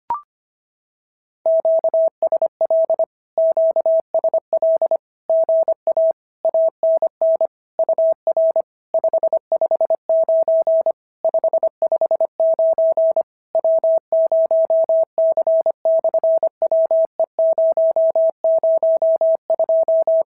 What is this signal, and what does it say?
Morse code, heard as a keyed tone, about 25 wpm, 650 Hz. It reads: QSL QSL GA ANN UR 559 559 W0C/WE002
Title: SOTA QSO Practice 25wpm